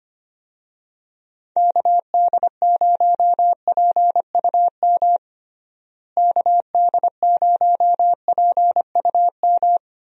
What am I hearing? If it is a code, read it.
Morse code, heard as a keyed tone, about 25 wpm, 700 Hz. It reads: KD0PUM KD0PUM
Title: USA Call Signs - 6 Characters - Repeated 3x - 25wpm